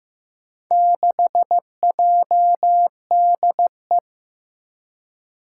Morse code, heard as a keyed tone, about 15 wpm, 700 Hz. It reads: 6JDE
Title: Four Challenging Letter-Number Combinations 15wpm